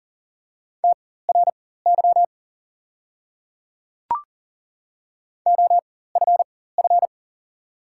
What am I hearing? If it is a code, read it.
Morse code, heard as a keyed tone, about 40 wpm, spaced as for 19 wpm, 700 Hz. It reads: TRY OFF
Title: 4x Character Spacing - Top 500 Words - 40wpm